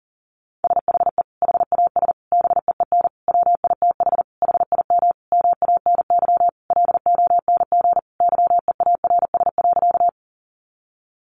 Morse code, heard as a keyed tone, about 40 wpm, 700 Hz. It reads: SHE HAS BEEN WITH HIM MANY LONG YEARS.
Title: Sentences from Top 200 Words 40wpm